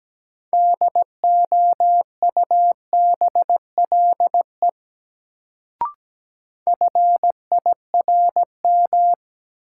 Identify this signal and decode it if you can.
Morse code, heard as a keyed tone, about 17 wpm, 700 Hz. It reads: DOUBLE FIRM
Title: Sentences from Top 1400 Words - New Word Drill Alt - 17wpm